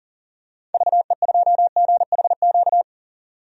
Morse code, heard as a keyed tone, about 40 wpm, 700 Hz. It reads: VE2GHQ
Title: DXCC-100 Call Signs - 6 Characters - 40wpm